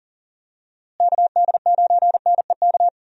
Morse code, heard as a keyed tone, about 39 wpm, 700 Hz. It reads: KD9NEK
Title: USA Call Signs - 6 Characters - Repeated 3x - 40wpm